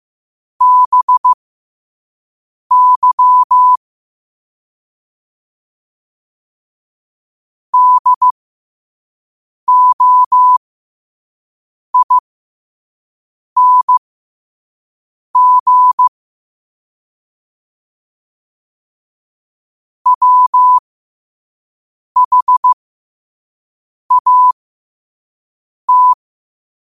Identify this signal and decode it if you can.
Morse code, heard as a keyed tone, about 15 wpm, spaced as for 5 wpm, 1000 Hz. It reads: BY DOING WHAT
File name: qotdfiles / 5wpm